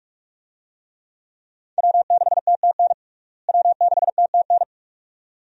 Morse code, heard as a keyed tone, about 42 wpm, 700 Hz. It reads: W6TTN W6TTN
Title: USA Call Signs - 5 Characters - Repeated 2x - 45wpm